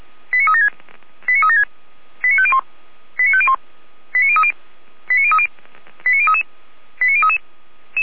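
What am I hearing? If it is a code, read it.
Five-tone selective call (ZVEI), heard as quick runs of five tones, tones of about 70 ms. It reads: 89376 89376 89501 89501 89030 89030 89033 89033
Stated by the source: Five-tone ZVEI selective call received by SDR, fire brigade Orlamünde (feuerwehr-orlamuende).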